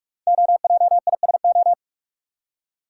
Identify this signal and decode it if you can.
Morse code, heard as a keyed tone, about 44 wpm, 690 Hz. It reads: OJISO
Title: Code Groups - 5 Letters - 45wpm